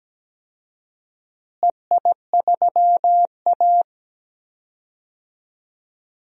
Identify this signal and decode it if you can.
Morse code, heard as a keyed tone, about 17 wpm, 700 Hz. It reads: EI3A
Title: DXCC-100 Call Signs - 4 Characters - Repeated 3x - 17wpm